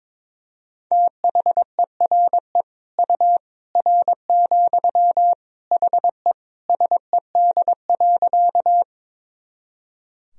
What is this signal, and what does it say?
Morse code, heard as a keyed tone, about 22 wpm, 700 Hz. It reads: THERE U R, HE SED.
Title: Sentences from Top 100 Words and Abbreviations 22wpm